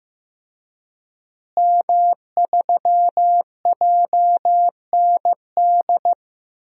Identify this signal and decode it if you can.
Morse code, heard as a keyed tone, about 16 wpm, 700 Hz. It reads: M3JND
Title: DXCC-100 Call Signs - 5 Characters - Repeated 3x - 15wpm